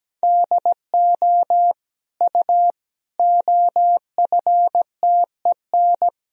Morse code, heard as a keyed tone, about 17 wpm, 700 Hz. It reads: DO U OFTEN
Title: Sentences from Top 300 Words and Abbreviations 17wpm